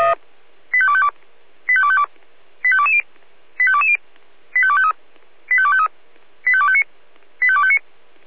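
Five-tone selective call (ZVEI), heard as quick runs of five tones, tones of about 70 ms. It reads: 85252 85252 85220 85220 85253 85253 85259 85259
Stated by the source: ZVEI five-tone selective call 85218 received with an SDR.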